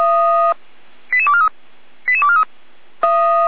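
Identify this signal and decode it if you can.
Five-tone selective call (ZVEI), heard as quick runs of five tones, tones of about 70 ms. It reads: 88243 88243
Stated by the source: ZVEI five-tone call 88331 received by SDR, archive path 455.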